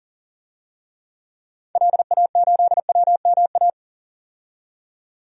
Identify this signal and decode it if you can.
Morse code, heard as a keyed tone, about 40 wpm, 700 Hz. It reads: LA8WMA
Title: DXCC-100 Call Signs - 6 Characters - 40wpm